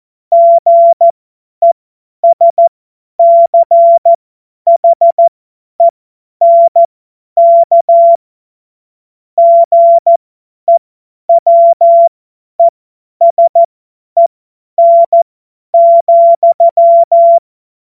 Morse code, heard as a keyed tone, about 14 wpm, spaced as for 10 wpm, 680 Hz. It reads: GESCHENK GEWESEN,